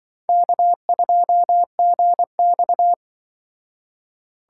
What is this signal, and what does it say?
Morse code, heard as a keyed tone, about 23 wpm, 700 Hz. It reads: K2GX
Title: Speed Racing - USA 1x2 Call Signs 20wpm